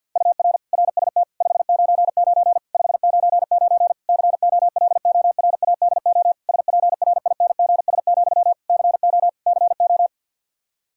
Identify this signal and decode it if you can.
Morse code, heard as a keyed tone, about 52 wpm, 700 Hz. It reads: UR RST 599 599 COLORADO SPRINGS, CO CO
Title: K1USN SST QSO Practice 50wpm